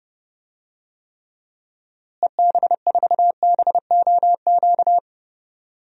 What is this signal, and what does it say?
Morse code, heard as a keyed tone, about 31 wpm, 700 Hz. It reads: EB4BOQ
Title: DXCC-100 Call Signs - 6 Characters - Repeated 3x - 30wpm